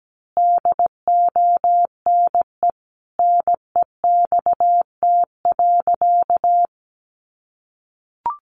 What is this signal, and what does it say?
Morse code, heard as a keyed tone, about 17 wpm, 700 Hz. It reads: DONE NEXT.
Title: Sentences from Top 1200 Words 17wpm